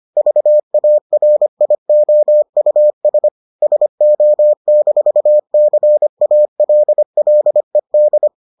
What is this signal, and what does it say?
Morse code, heard as a keyed tone, about 25 wpm, 600 Hz. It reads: VARIOUS SO-CALLED